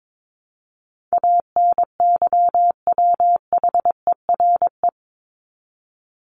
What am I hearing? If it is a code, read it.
Morse code, heard as a keyed tone, about 22 wpm, 700 Hz. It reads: ANYWHERE
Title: Top 1251-1500 Words 22wpm